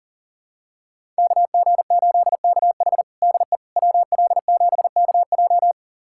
Morse code, heard as a keyed tone, about 40 wpm, 700 Hz. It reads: KG8KH DE WL7KJ
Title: USA Callsign DE Callsign - 5 Characters - 40wpm